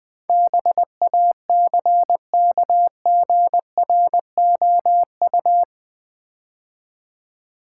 Morse code, heard as a keyed tone, about 20 wpm, 700 Hz. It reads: BACKGROU
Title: Build A Word - 4 Letters At A Time - 10-Letter Words - 20wpm